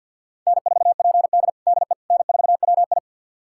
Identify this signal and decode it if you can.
Morse code, heard as a keyed tone, about 52 wpm, 700 Hz. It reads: N4PD DE N4WI
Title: USA Callsign DE Callsign - 4 Characters - 50wpm